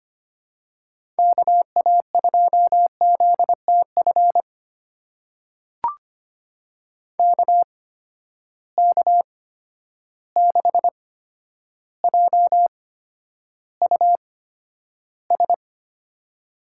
Morse code, heard as a keyed tone, about 25 wpm, spaced as for 11 wpm, 700 Hz. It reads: KA2ZTF KK6JUS
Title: USA Call Signs - 6 Characters - 8x Character Spacing - 25wpm